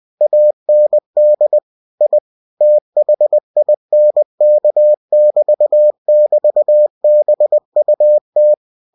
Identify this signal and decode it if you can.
Morse code, heard as a keyed tone, about 20 wpm, 600 Hz. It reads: AND I THINK<BT><BT>BUT